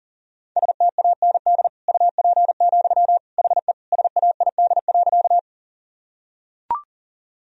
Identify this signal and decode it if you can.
Morse code, heard as a keyed tone, about 37 wpm, 700 Hz. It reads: STAND UP, HE SAID.
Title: Sentences from Top 1000 Words - Review Part 1 - 40wpm